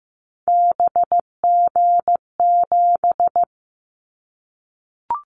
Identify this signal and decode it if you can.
Morse code, heard as a keyed tone, about 15 wpm, 700 Hz. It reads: BG7
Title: Callsign Prefixes for Top 100 DXCC - 15wpm